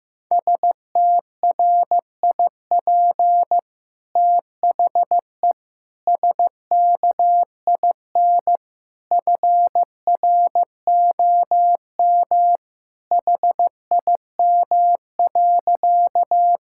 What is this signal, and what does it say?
Morse code, heard as a keyed tone, about 15 wpm, 700 Hz. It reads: STRIP THE SKIN FROM HIM.
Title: Sentences from Top 1600 Words 15wpm